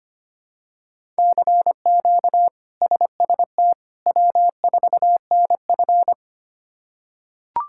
Morse code, heard as a keyed tone, about 25 wpm, 700 Hz. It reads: CQ SST W4NF
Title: K1USN SST QSO Practice - 25wpm